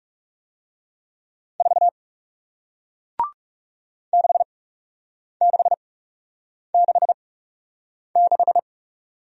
Morse code, heard as a keyed tone, about 41 wpm, spaced as for 8 wpm, 700 Hz. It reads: 4 6666